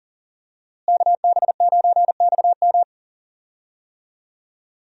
Morse code, heard as a keyed tone, about 41 wpm, 700 Hz. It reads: KB9XM
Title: USA Call Signs - 5 Characters - Repeated 3x - 40wpm